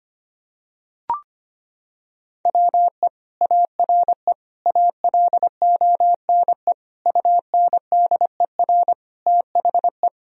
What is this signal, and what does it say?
Morse code, heard as a keyed tone, about 25 wpm, 700 Hz. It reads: WE ARE ALONE UNDER THE
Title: Sentences from Top 1000 Words - Review Part 1 - 25wpm